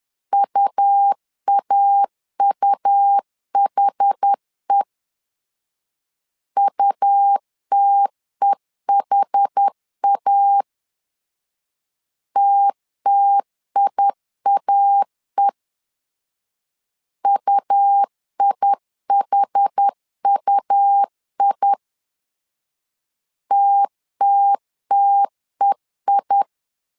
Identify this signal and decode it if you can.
Morse code, heard as a keyed tone, about 10 wpm, 800 Hz. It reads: UAUHE UTEHA TTIAE UIHUI TTTEI
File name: LEZIONI / 2 UHDN / 2 UH 10wpm 1714